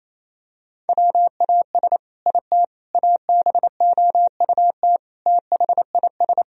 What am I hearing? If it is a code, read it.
Morse code, heard as a keyed tone, about 27 wpm, 700 Hz. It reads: WAS IT ABOUT THIS
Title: Sentences from Top 1000 Words - Review Part 3 - 28wpm